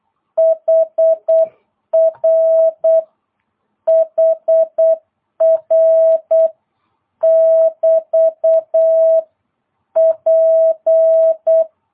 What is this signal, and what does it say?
Morse code, heard as a keyed tone, about 8 wpm, 650 Hz. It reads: HR HR <BT> P
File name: jh8cbh / 2022cw